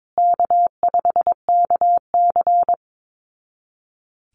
Morse code, heard as a keyed tone, about 22 wpm, 700 Hz. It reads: K5KC
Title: Easy Call Signs Repeated 2x - 22wpm